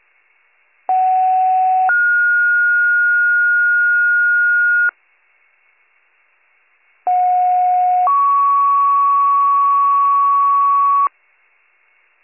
Two-tone page, heard as a one-second tone, then a three-second tone, about 750 then 1430 Hz, about 730 then 1120 Hz. It is heard from 0.9 to 11.1 s.